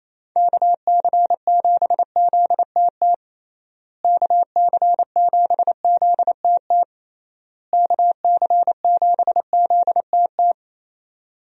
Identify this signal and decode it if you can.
Morse code, heard as a keyed tone, about 28 wpm, 700 Hz. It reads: KC7ZTT KC7ZTT KC7ZTT
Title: USA Call Signs - 6 Characters - Repeated 3x - 28wpm